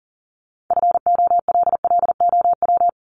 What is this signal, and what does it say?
Morse code, heard as a keyed tone, about 40 wpm, 700 Hz. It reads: FOLLOW
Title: ING Suffix - Top 100 Words with Base Words 40wpm